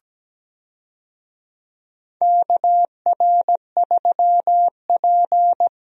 Morse code, heard as a keyed tone, about 17 wpm, 700 Hz. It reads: KR3P
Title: USA Call Signs - 4 Characters - Repeated 2x - 17wpm